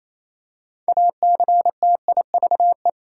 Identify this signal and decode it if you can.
Morse code, heard as a keyed tone, about 28 wpm, 700 Hz. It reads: ACTIVE
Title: Six Letter Words - 28wpm